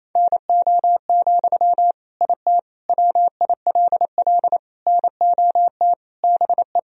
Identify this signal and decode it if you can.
Morse code, heard as a keyed tone, about 28 wpm, 700 Hz. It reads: NO, IT WILL NOT BE
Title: Sentences from Top 1000 Words - Review Part 4 - 28wpm